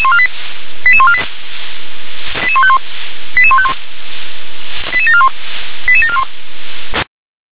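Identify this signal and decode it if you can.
Five-tone selective call (ZVEI), heard as quick runs of five tones, tones of about 70 ms. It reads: 88148 88151 88151 88741 88741
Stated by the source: ZVEI five-tone selective call received on SDR.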